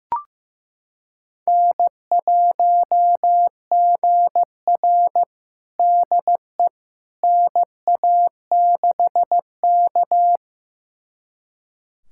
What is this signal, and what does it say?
Morse code, heard as a keyed tone, about 15 wpm, 700 Hz. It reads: N1GR DE NA6K
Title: USA Callsign DE Callsign - 4 Characters - 15wpm